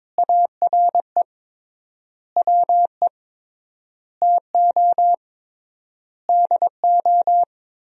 Morse code, heard as a keyed tone, about 22 wpm, 700 Hz. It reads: ARE WE TO DO
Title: 3x Word-Spacing - Sentences from Top 100 Words - 22wpm